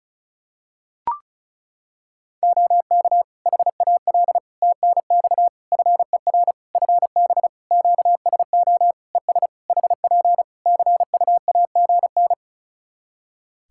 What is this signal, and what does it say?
Morse code, heard as a keyed tone, about 34 wpm, 700 Hz. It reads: OK HAL TNX FER FB QSO ES HP CUAGN
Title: Practice QSO - Exchange 4 - Ending - 35wpm